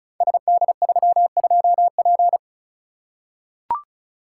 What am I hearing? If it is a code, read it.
Morse code, heard as a keyed tone, about 36 wpm, 690 Hz. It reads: SD32P
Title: Code Groups - 5 Letter:Numbers - 35wpm